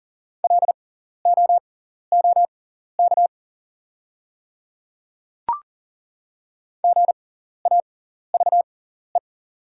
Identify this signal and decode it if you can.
Morse code, heard as a keyed tone, about 39 wpm, spaced as for 14 wpm, 690 Hz. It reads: LOOK GAVE
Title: 6x Character Spacing - Top 500 Words - 40wpm